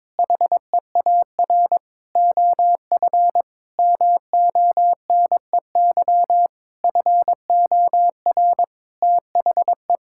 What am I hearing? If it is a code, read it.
Morse code, heard as a keyed tone, about 22 wpm, 700 Hz. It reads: HEAR OF MONEY FOR THE
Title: Sentences from Top 1000 Words - Review Part 5 - 22wpm